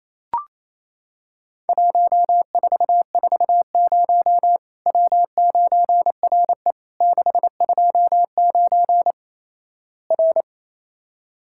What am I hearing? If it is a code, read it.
Morse code, heard as a keyed tone, about 28 wpm, 700 Hz. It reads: 1440 W9RE 629 R
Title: Worked All Europe DX Contest - QTC Exchange - 28wpm